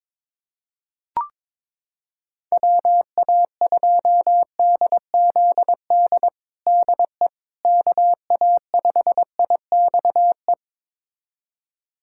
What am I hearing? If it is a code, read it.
Morse code, heard as a keyed tone, about 21 wpm, 700 Hz. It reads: WA2DZD DE KA5IXE
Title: USA Callsign DE Callsign - 6 Characters - 22wpm